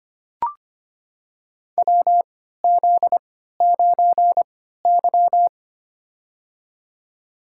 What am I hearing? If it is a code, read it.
Morse code, heard as a keyed tone, about 25 wpm, spaced as for 14 wpm, 700 Hz. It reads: WZ9Y
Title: USA Call Signs - 4 Characters - 3x Character Spacing - 25wpm